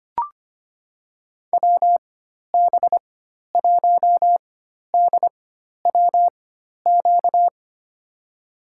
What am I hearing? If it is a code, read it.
Morse code, heard as a keyed tone, about 25 wpm, spaced as for 12 wpm, 700 Hz. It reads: WB1DWQ
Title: USA Call Signs - 6 Characters - 4x Character Spacing - 25wpm